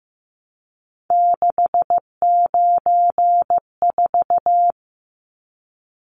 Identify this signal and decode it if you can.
Morse code, heard as a keyed tone, about 15 wpm, 700 Hz. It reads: 694
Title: Numbers - 3-Digit 15wpm